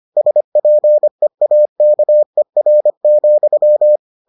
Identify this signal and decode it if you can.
Morse code, heard as a keyed tone, about 25 wpm, 600 Hz. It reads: SPEAKER,